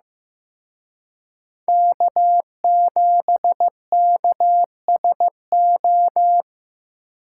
Morse code, heard as a keyed tone, about 18 wpm, 700 Hz. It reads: K7KSO
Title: Speed Racing - USA 1x3 Call Signs 15wpm